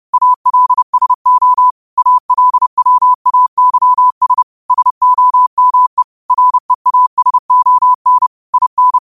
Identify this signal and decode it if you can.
Morse code, heard as a keyed tone, about 30 wpm, 1000 Hz. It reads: ALSO ALWAYS SOME REASON IN